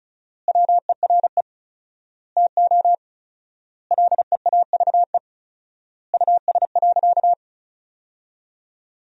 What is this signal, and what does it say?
Morse code, heard as a keyed tone, about 34 wpm, 700 Hz. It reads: WERE TO LEAVE US.
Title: 4x Word-Spacing - Sentences from Top 400 Words - 35wpm